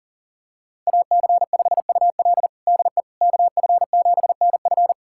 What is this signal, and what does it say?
Morse code, heard as a keyed tone, about 40 wpm, 700 Hz. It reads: AC5UL DE KF7NF
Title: USA Callsign DE Callsign 40wpm